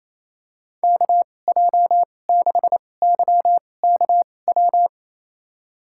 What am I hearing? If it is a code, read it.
Morse code, heard as a keyed tone, about 27 wpm, spaced as for 20 wpm, 700 Hz. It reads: KJ6YKW
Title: USA Call Signs - 6 Characters - 2x Character Spacing - 28wpm